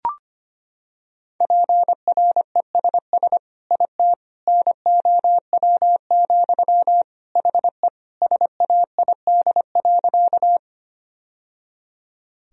Morse code, heard as a keyed tone, about 25 wpm, 700 Hz. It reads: PRESS IT NOW, HE SAID.
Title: Sentences from Top 1000 Words - Review Part 1 - 25wpm